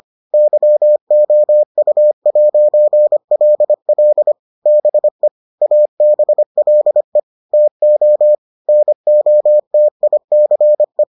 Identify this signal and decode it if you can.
Morse code, heard as a keyed tone, about 25 wpm, 600 Hz. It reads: YOU'LL BE ABLE TO NOTICE